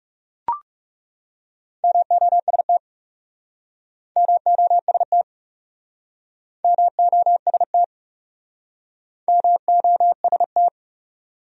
Morse code, heard as a keyed tone, about 37 wpm, 700 Hz. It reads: MOST MOST MOST MOST
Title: Speed-Racing Top 200 Words 30wpm